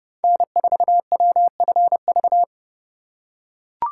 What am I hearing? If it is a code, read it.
Morse code, heard as a keyed tone, about 30 wpm, 700 Hz. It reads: N4WFV
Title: USA Call Signs - 4x Character Spacing - 30wpm